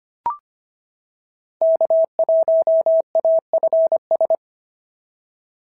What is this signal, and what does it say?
Morse code, heard as a keyed tone, about 25 wpm, 690 Hz. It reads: K1AFS
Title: SOTA QSO Practice 25wpm